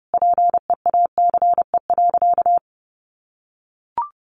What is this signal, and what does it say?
Morse code, heard as a keyed tone, about 30 wpm, 700 Hz. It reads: PEACE.
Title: Sentences from Top 1500 Words 30wpm